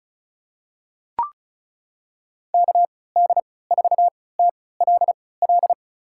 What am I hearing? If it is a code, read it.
Morse code, heard as a keyed tone, about 36 wpm, spaced as for 20 wpm, 700 Hz. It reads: KD4TLL
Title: USA Call Signs - 3x Character Spacing - 35wpm